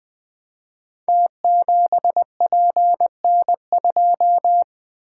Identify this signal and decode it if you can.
Morse code, heard as a keyed tone, about 20 wpm, 700 Hz. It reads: T7PN2
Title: Code Groups - 5 Letter:Numbers - 20wpm